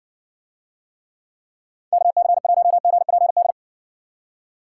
Morse code, heard as a keyed tone, about 60 wpm, 700 Hz. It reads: KC1GPD